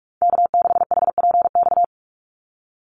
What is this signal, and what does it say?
Morse code, heard as a keyed tone, about 44 wpm, 700 Hz. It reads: K6HPX
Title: NA SOTA Activator Callsigns 45wpm